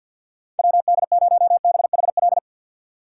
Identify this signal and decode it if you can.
Morse code, heard as a keyed tone, about 51 wpm, 690 Hz. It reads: WD0BHL
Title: USA Call Signs - 2x Character Spacing - 50wpm